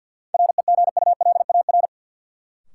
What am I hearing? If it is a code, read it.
Morse code, heard as a keyed tone, about 51 wpm, 700 Hz. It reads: REGULAR
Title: Sentences from Top 900 Words - New Word Drill - 50wpm